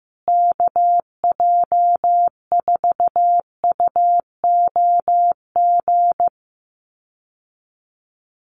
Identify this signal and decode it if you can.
Morse code, heard as a keyed tone, about 15 wpm, 700 Hz. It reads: KJ4UOG
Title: USA Call Signs Repeated 3x 15wpm